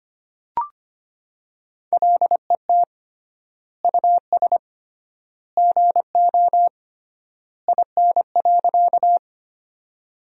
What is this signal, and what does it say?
Morse code, heard as a keyed tone, about 25 wpm, 700 Hz. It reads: LET US GO IN.
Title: 3x Word-Spacing - Sentences from Top 300 Words - 25wpm